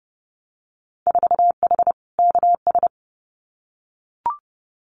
Morse code, heard as a keyed tone, about 30 wpm, 700 Hz. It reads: 4H KS
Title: Winter Field Day Exchanges Encore 30wpm